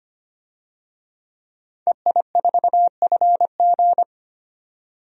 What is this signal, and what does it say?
Morse code, heard as a keyed tone, about 25 wpm, 690 Hz. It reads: EI4FG
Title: DXCC-100 Call Signs - 5 Characters - 25wpm